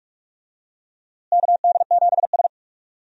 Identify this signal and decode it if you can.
Morse code, heard as a keyed tone, about 45 wpm, 700 Hz. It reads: KD7S
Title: USA Call Signs - 4 Characters - Repeated 3x - 45wpm